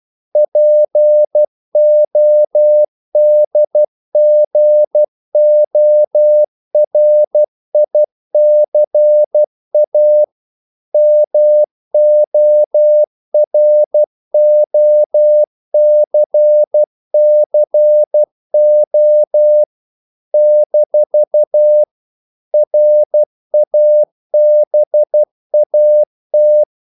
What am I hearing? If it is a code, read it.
Morse code, heard as a keyed tone, about 12 wpm, 600 Hz. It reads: PODGORICA MOROCCO - RABAT
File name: CW / Practice / NationalCapitals12